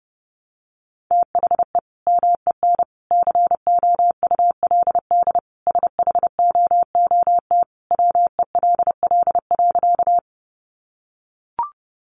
Morse code, heard as a keyed tone, about 30 wpm, 700 Hz. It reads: THE MEN COULD SHOOT WELL.
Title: Sentences from Top 1200 Words 30wpm